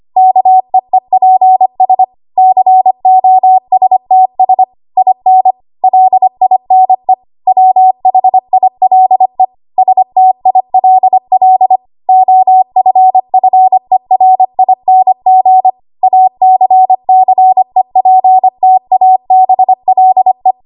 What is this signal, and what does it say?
Morse code, heard as a keyed tone, about 25 wpm, 750 Hz. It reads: KEEPS COSTS IN LINE WHILE STILL OFFERING ACCEPTABLE
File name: Archive / 25 WPM / 25WPM